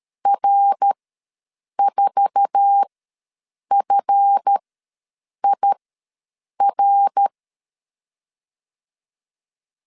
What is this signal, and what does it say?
Morse code, heard as a keyed tone, about 13 wpm, spaced as for 7 wpm, 800 Hz. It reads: R4FIR
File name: LEZIONI / 8 YJ45 / 8 YJ45 ws 12wpm 3146